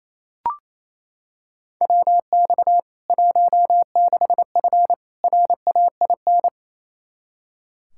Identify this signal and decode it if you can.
Morse code, heard as a keyed tone, about 28 wpm, 700 Hz. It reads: WX 16F RAIN
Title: K1USN SST QSO Practice 28wpm